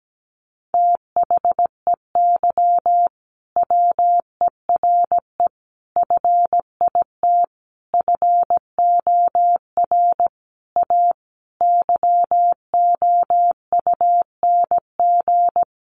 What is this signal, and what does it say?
Morse code, heard as a keyed tone, about 17 wpm, 700 Hz. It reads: THEY WERE FIT FOR A YOUNG
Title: Sentences from Top 800 Words 17wpm